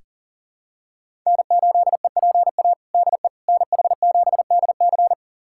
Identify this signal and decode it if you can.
Morse code, heard as a keyed tone, about 40 wpm, 700 Hz. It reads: N8EPA DE NH7DC
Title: USA Callsign DE Callsign - 5 Characters - 40wpm